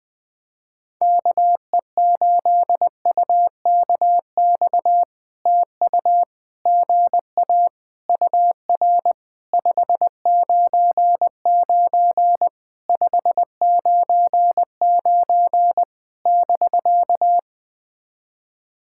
Morse code, heard as a keyed tone, about 20 wpm, 700 Hz. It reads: KE8UKX TU GA UR 599 599 <BK>
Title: POTA QSO Practice - 20wpm